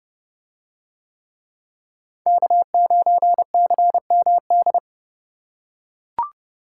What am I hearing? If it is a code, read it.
Morse code, heard as a keyed tone, about 29 wpm, 700 Hz. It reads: K9CMD